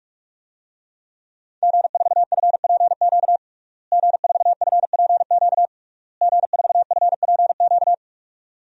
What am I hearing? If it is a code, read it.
Morse code, heard as a keyed tone, about 42 wpm, 700 Hz. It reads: G4FPQ G4FPQ G4FPQ
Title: DXCC-100 Call Signs - 5 Characters - Repeated 3x - 45wpm